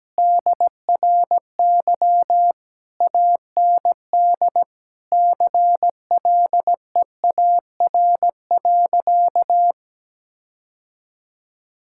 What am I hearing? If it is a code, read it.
Morse code, heard as a keyed tone, about 17 wpm, 700 Hz. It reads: DRY AND CLEAR.
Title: Sentences from Top 1600 Words 17wpm